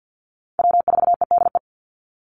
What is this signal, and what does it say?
Morse code, heard as a keyed tone, about 50 wpm, 700 Hz. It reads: W4EDE
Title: Call Signs Repeated 2x - 50wpm